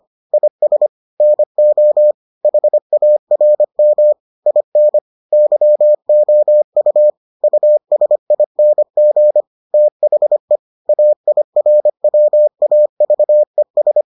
Morse code, heard as a keyed tone, about 25 wpm, 600 Hz. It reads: IS NO HARM IN YOU USING THE AIRWAVES